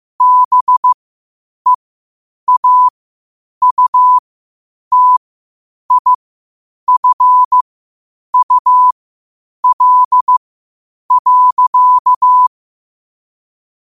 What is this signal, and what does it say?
Morse code, heard as a keyed tone, about 15 wpm, spaced as for 8 wpm, 1000 Hz. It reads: BEAUTIFUL.
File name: qotdfiles / 8wpm